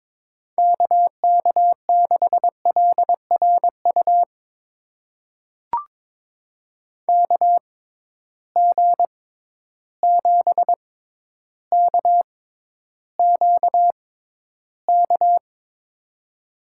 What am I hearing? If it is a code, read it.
Morse code, heard as a keyed tone, about 22 wpm, spaced as for 11 wpm, 700 Hz. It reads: KK6LRU KG7KQK
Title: USA Call Signs - 6 Characters - 6x Character Spacing - 22wpm